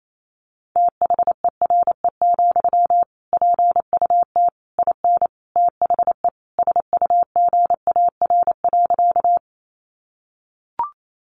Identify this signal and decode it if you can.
Morse code, heard as a keyed tone, about 28 wpm, 700 Hz. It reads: THERE, PUT IN THE SUGAR.
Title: Sentences from Top 1200 Words 28wpm